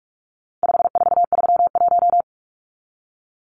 Morse code, heard as a keyed tone, about 44 wpm, 700 Hz. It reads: 5431
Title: Numbers - 4-Digit 45wpm